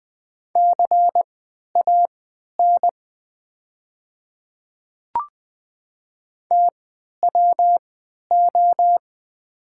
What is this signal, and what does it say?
Morse code, heard as a keyed tone, about 20 wpm, spaced as for 11 wpm, 700 Hz. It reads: CAN TWO
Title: 3x Character Spacing - Top 100 Words - 20wpm